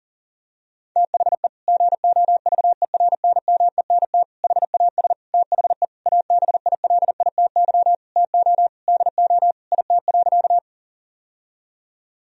Morse code, heard as a keyed tone, about 40 wpm, 700 Hz. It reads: THE GOVERNMENT HAS THE ABILITY TO DO IT.
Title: Sentences from Top 1000 Words - Review Part 3 - 40wpm